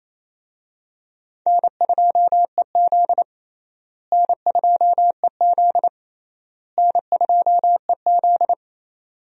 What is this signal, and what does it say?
Morse code, heard as a keyed tone, about 28 wpm, 700 Hz. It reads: N2EZ N2EZ N2EZ
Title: USA Call Signs - 4 Characters - Repeated 3x - 28wpm